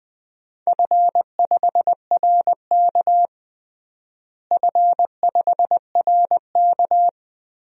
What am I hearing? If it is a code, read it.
Morse code, heard as a keyed tone, about 20 wpm, 700 Hz. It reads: F5RK F5RK
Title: DXCC-100 Call Signs - 4 Characters - Repeated 3x - 20wpm